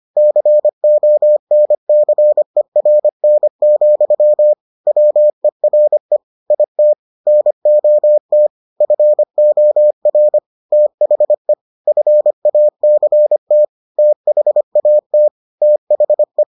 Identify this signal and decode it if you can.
Morse code, heard as a keyed tone, about 25 wpm, 600 Hz. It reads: CONCERN, WERE IT NOT FOR THE FACT THAT THE